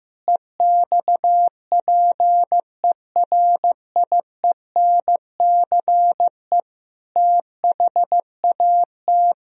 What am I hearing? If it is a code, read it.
Morse code, heard as a keyed tone, about 16 wpm, 700 Hz. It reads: EXPERIENCE THAT
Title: Sentences from Top 1000 Words - Review Part 4 - 15wpm